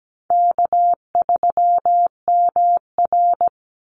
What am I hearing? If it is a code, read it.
Morse code, heard as a keyed tone, about 17 wpm, 700 Hz. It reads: K3MR
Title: Call Signs Repeated 3x - 17wpm